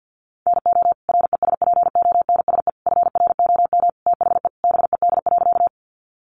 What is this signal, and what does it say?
Morse code, heard as a keyed tone, about 50 wpm, 700 Hz. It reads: NO RESPONSE FROM THE BED.
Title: Sentences from Top 1400 Words 50wpm